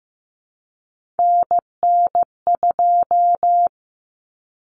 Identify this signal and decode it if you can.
Morse code, heard as a keyed tone, about 15 wpm, 700 Hz. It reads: NN2
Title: Common Cut Numbers - 3-Digit 15wpm